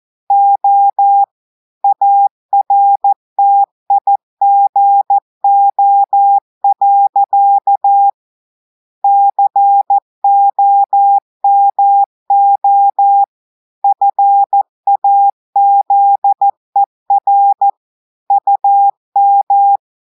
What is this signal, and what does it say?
Morse code, heard as a keyed tone, about 14 wpm, 800 Hz. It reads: O ARTIGO. COMO FAZER UM
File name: audios / portugais / portugais texte en clair 2 vitesse 14 mots minutes